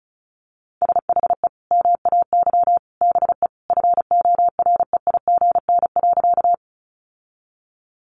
Morse code, heard as a keyed tone, about 34 wpm, 700 Hz. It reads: SHE MAY BE FOREIGN.
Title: Sentences from Top 1400 Words 35wpm